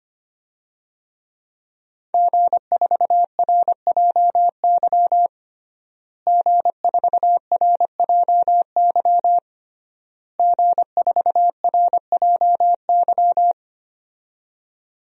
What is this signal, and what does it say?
Morse code, heard as a keyed tone, about 25 wpm, 700 Hz. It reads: G4RJY G4RJY G4RJY
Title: DXCC-100 Call Signs - 5 Characters - Repeated 3x - 25wpm